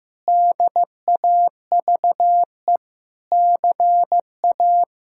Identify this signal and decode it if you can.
Morse code, heard as a keyed tone, about 15 wpm, 700 Hz. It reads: DAVE CA
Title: K1USN SST QSO Practice - 15wpm